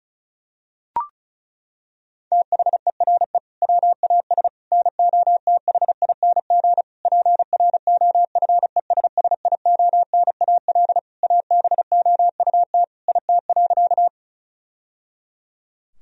Morse code, heard as a keyed tone, about 36 wpm, 700 Hz. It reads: THERE WAS NOTHING PROFESSIONAL ABOUT IT.
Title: Sentences from Top 1000 Words - Review Part 4 - 35wpm